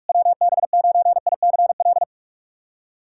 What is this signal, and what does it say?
Morse code, heard as a keyed tone, about 48 wpm, 700 Hz. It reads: WB9ICL
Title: SOTA QSO Practice 45wpm